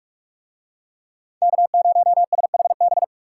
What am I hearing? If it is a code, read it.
Morse code, heard as a keyed tone, about 44 wpm, 700 Hz. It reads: K0SHB